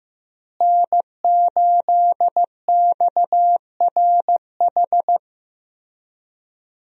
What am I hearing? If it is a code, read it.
Morse code, heard as a keyed tone, about 15 wpm, 700 Hz. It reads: N8XRH
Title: USA Call Signs - 5 Characters - Repeated 3x - 15wpm